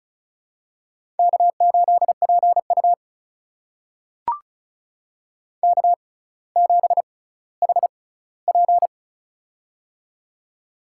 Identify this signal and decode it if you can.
Morse code, heard as a keyed tone, about 34 wpm, spaced as for 18 wpm, 700 Hz. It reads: K8PU K7HP
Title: USA Call Signs - 4 Characters - 6x Character Spacing - 35wpm